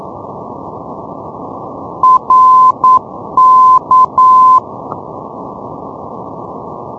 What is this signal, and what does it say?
Morse code, heard as a keyed tone, about 9 wpm, 1030 Hz. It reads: RK
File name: europe / isl / isl-620 ndb reykjavik rk-355-np-isl-2012-04-k-rb